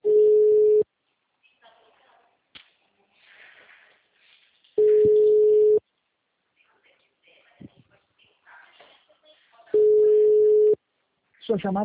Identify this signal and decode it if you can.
Ringback tone (Europe, 425 Hz), with speech.